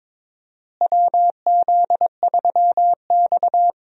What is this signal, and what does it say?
Morse code, heard as a keyed tone, about 22 wpm, 700 Hz. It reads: WZ3X
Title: USA Call Signs - 4 Characters - Repeated 2x - 22wpm